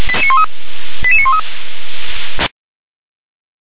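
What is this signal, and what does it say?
Five-tone selective call (ZVEI), heard as quick runs of five tones, tones of about 70 ms. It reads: 88913
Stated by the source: ZVEI five-tone selective call received by SDR.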